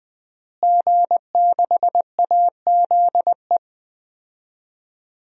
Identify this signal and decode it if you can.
Morse code, heard as a keyed tone, about 21 wpm, 700 Hz. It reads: G6AZE
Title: DXCC-100 Call Signs - 5 Characters - Repeated 2x - 20wpm